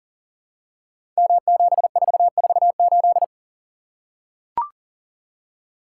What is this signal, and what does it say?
Morse code, heard as a keyed tone, about 42 wpm, 700 Hz. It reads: M7448
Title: Code Groups - 5 Letter:Numbers - 40wpm